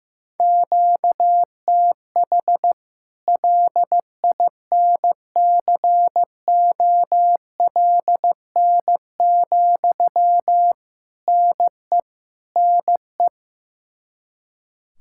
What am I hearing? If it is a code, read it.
Morse code, heard as a keyed tone, about 15 wpm, 700 Hz. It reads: QTH LINCOLN, NE NE
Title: Practice QSO - Exchange 1 - Beginning - 15wpm